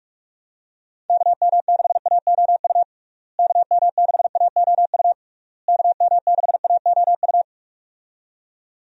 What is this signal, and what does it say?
Morse code, heard as a keyed tone, about 43 wpm, 700 Hz. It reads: KM6AOU KM6AOU KM6AOU
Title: USA Call Signs - 6 Characters - Repeated 3x - 45wpm